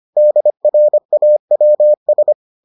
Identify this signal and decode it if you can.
Morse code, heard as a keyed tone, about 25 wpm, 600 Hz. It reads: DRAWS